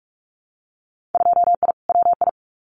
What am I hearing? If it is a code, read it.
Morse code, heard as a keyed tone, about 46 wpm, 700 Hz. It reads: 2I WI
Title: Winter Field Day Exchanges 45wpm